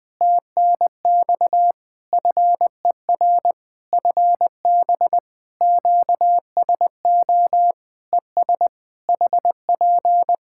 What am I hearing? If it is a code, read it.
Morse code, heard as a keyed tone, about 20 wpm, 700 Hz. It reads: TNX FER FB QSO ES HP
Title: Practice QSO - Exchange 4 - Ending - 20wpm